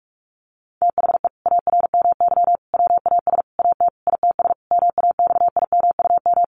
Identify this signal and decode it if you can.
Morse code, heard as a keyed tone, about 44 wpm, 700 Hz. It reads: THE ARMY WAS AT ITS MAXIMUM
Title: Sentences from Top 1300 Words 45wpm